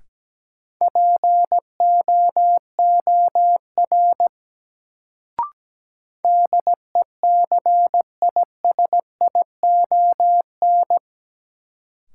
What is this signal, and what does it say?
Morse code, heard as a keyed tone, about 17 wpm, 700 Hz. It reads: POOR DECISION
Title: Sentences from Top 1000 Words - New Word Drill - 17wpm